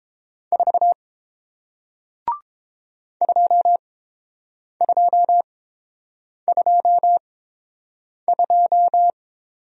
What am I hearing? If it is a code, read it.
Morse code, heard as a keyed tone, about 30 wpm, spaced as for 7 wpm, 700 Hz. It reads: 4 2222